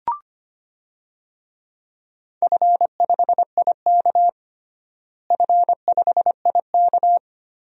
Morse code, heard as a keyed tone, about 25 wpm, 700 Hz. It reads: F5IK F5IK
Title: DXCC-100 Call Signs - 4 Characters - Repeated 2x - 25wpm